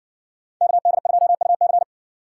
Morse code, heard as a keyed tone, about 57 wpm, 700 Hz. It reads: KD2UC